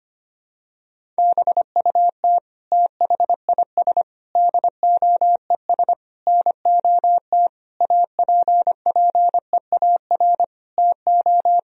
Morse code, heard as a keyed tone, about 25 wpm, 700 Hz. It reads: BUT THIS DOES NOT APPEAR TO
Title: Sentences from Top 1000 Words - Review Part 2 - 25wpm